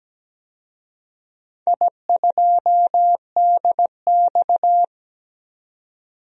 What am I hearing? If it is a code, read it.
Morse code, heard as a keyed tone, about 17 wpm, 700 Hz. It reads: I2DX
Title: DXCC-100 Call Signs - 4 Characters - 17wpm